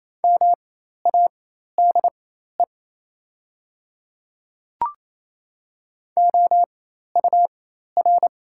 Morse code, heard as a keyed tone, about 28 wpm, spaced as for 13 wpm, 700 Hz. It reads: MADE OUR
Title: 4x Character Spacing - Top 500 Words - 28wpm